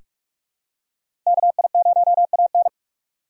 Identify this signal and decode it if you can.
Morse code, heard as a keyed tone, about 46 wpm, 700 Hz. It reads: KI0AN
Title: USA Call Signs - 2x Character Spacing - 45wpm